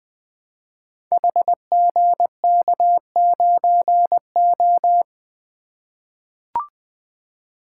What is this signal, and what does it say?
Morse code, heard as a keyed tone, about 20 wpm, 700 Hz. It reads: HGK9O
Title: Code Groups - 5 Letter:Numbers - 20wpm